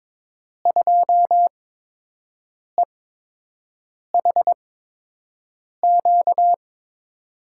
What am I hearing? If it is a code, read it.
Morse code, heard as a keyed tone, about 22 wpm, spaced as for 6 wpm, 700 Hz. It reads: 2EHQ
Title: USA Call Signs - 6 Characters - 8x Character Spacing - 22wpm